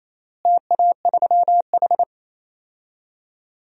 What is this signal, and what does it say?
Morse code, heard as a keyed tone, about 28 wpm, 700 Hz. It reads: TA3H
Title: DXCC-100 Call Signs - 4 Characters - Repeated 3x - 28wpm